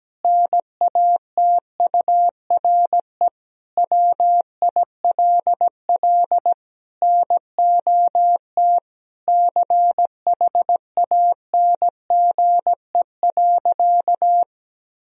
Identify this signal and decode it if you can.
Morse code, heard as a keyed tone, about 17 wpm, 700 Hz. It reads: NATURE WILL NOT CHANGE.
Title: Sentences from Top 1000 Words - Review Part 3 - 17wpm